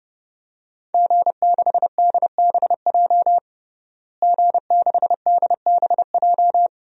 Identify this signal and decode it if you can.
Morse code, heard as a keyed tone, about 30 wpm, 700 Hz. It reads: G6DBJ G6DBJ
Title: DXCC-100 Call Signs - 5 Characters - Repeated 3x - 30wpm